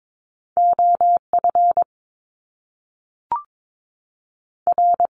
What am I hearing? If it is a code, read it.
Morse code, heard as a keyed tone, about 22 wpm, 700 Hz. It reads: OF R
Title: Top Two Letter Combinations in Words 22wpm